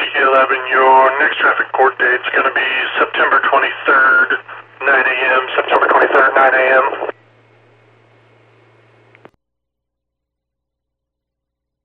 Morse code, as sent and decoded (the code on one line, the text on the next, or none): none